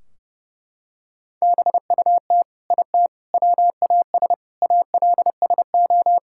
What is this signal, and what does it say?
Morse code, heard as a keyed tone, about 30 wpm, 700 Hz. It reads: BUT IT WAS ALSO
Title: Sentences from Top 1600 Words 30wpm